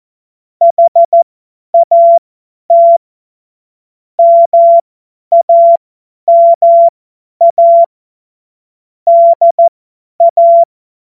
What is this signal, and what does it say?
Morse code, heard as a keyed tone, about 14 wpm, spaced as for 10 wpm, 680 Hz. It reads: HAT MAMA DA
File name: Morsekurs / Witze / Witz 02. Februar 2021, 50 F70